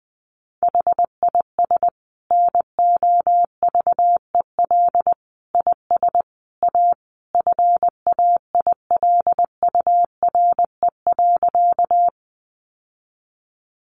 Morse code, heard as a keyed tone, about 20 wpm, 700 Hz. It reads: HIS NOVEL IS A FAILURE.
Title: Sentences from Top 1300 Words 20wpm